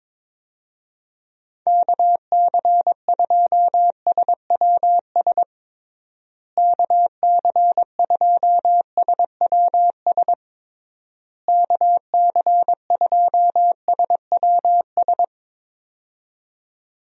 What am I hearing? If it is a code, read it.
Morse code, heard as a keyed tone, about 22 wpm, 700 Hz. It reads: KC2SWS KC2SWS KC2SWS
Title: USA Call Signs - 6 Characters - Repeated 3x - 22wpm